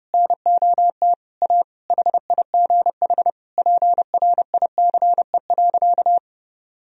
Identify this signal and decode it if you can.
Morse code, heard as a keyed tone, about 32 wpm, 700 Hz. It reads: NOT A HIGH PRICE.
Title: Sentences from Top 1000 Words - Review Part 4 - 30wpm